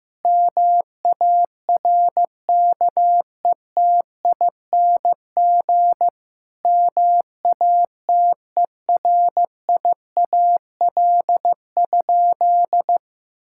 Morse code, heard as a keyed tone, about 16 wpm, 700 Hz. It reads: MARKETING MATERIAL?
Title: Sentences from Top 1000 Words - Review Part 5 - 15wpm